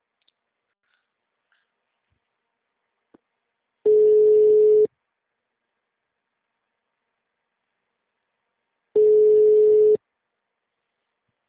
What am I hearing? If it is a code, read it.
Ringback tone (Europe, 425 Hz).
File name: out- -2007- - - .15971